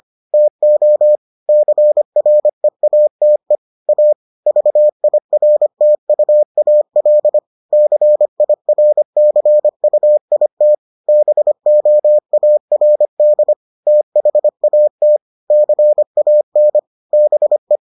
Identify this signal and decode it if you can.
Morse code, heard as a keyed tone, about 25 wpm, 600 Hz. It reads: TO CREATE A VIRTUAL CIRCUIT BOARD THAT CAN BE